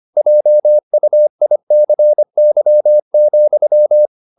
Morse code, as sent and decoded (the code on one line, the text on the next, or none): .--- ..- .. -.-. -.-- --..--
JUICY,